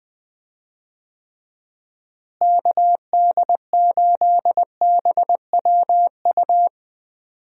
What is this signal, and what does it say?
Morse code, heard as a keyed tone, about 20 wpm, 700 Hz. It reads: KD8BWU
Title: USA Call Signs - 6 Characters - Repeated 2x - 20wpm